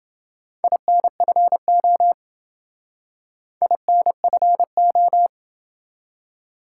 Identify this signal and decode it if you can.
Morse code, heard as a keyed tone, about 28 wpm, 700 Hz. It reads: INFO INFO